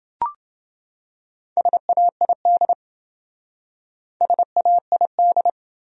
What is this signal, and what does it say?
Morse code, heard as a keyed tone, about 27 wpm, 700 Hz. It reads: SAID SAID